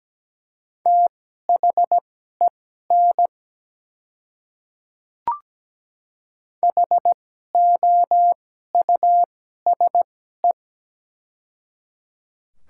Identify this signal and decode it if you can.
Morse code, heard as a keyed tone, about 17 wpm, spaced as for 12 wpm, 700 Hz. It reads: THEN HOUSE
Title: 2x Character Spacing - Top 200 Words - 17wpm